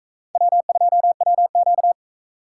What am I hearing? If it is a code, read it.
Morse code, heard as a keyed tone, about 41 wpm, 700 Hz. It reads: W2WQ